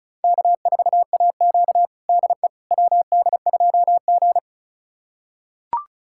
Morse code, heard as a keyed tone, about 36 wpm, 700 Hz. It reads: K4AQ DE WD2G
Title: USA Callsign DE Callsign - 4 Characters - 35wpm